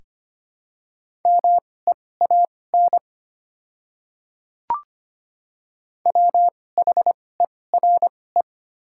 Morse code, heard as a keyed tone, about 25 wpm, spaced as for 18 wpm, 690 Hz. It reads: MEAN WHERE
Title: 2x Character Spacing - Top 200 Words - 25wpm